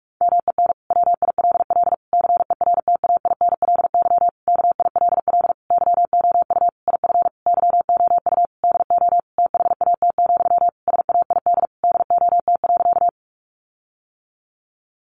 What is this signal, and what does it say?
Morse code, heard as a keyed tone, about 44 wpm, 700 Hz. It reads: MEN WILL CERTAINLY KILL YOU IF YOU DO THAT, SAID DOT.
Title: Sentences from Top 1500 Words 45wpm